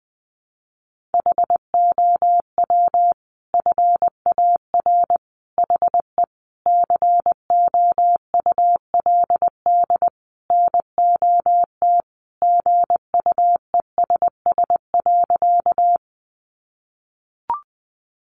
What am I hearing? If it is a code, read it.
Morse code, heard as a keyed tone, about 20 wpm, 700 Hz. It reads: HOW FAR HE COULD NOT GUESS.
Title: Sentences from Top 1500 Words 20wpm